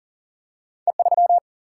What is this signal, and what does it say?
Morse code, heard as a keyed tone, about 42 wpm, 700 Hz. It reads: E3
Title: Single-Letter Single-Number 40wpm